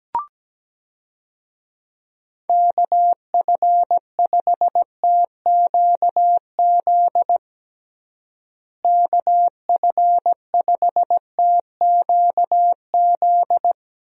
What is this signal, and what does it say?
Morse code, heard as a keyed tone, about 17 wpm, 700 Hz. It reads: KF5TQZ KF5TQZ
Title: USA Call Signs - 6 Characters - Repeated 2x - 17wpm